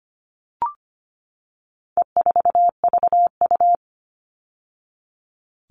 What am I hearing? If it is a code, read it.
Morse code, heard as a keyed tone, about 25 wpm, 700 Hz. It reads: E4VU
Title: All Cut Numbers - 4-Digit 25wpm